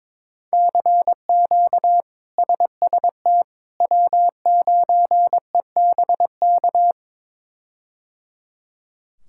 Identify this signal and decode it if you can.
Morse code, heard as a keyed tone, about 22 wpm, 700 Hz. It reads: CQ SST W9EBK
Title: K1USN SST QSO Practice - 22wpm